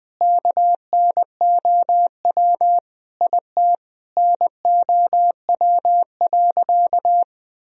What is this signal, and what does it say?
Morse code, heard as a keyed tone, about 20 wpm, 700 Hz. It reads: KNOW IT NOW.
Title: Sentences from Top 1000 Words - Review Part 4 - 20wpm